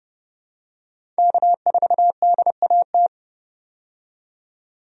Morse code, heard as a keyed tone, about 30 wpm, 700 Hz. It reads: K4DAT
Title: USA Call Signs - 5 Characters - 30wpm